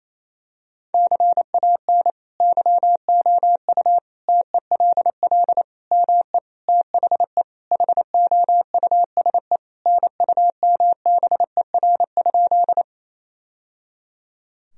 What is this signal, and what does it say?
Morse code, heard as a keyed tone, about 28 wpm, 700 Hz. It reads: CAN YOU TELL ME THE HOUSE NUMBER?
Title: Sentences from Top 1000 Words - Review Part 1 - 28wpm